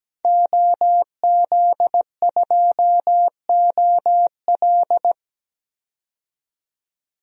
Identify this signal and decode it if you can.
Morse code, heard as a keyed tone, about 17 wpm, 700 Hz. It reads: OZ2OL
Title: DXCC-100 Call Signs - 5 Characters - 17wpm